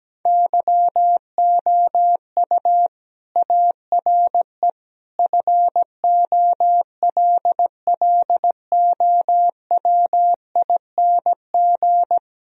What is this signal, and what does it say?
Morse code, heard as a keyed tone, about 17 wpm, 700 Hz. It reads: YOU ARE FOLLOWING
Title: Sentences from Top 1000 Words - Review Part 4 - 17wpm